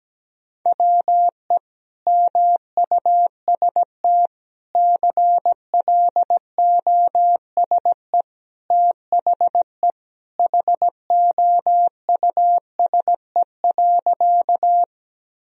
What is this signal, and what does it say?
Morse code, heard as a keyed tone, about 17 wpm, 700 Hz. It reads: WE MUST CLOSE THE HOUSE.
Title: Sentences from Top 1000 Words - Review Part 1 - 17wpm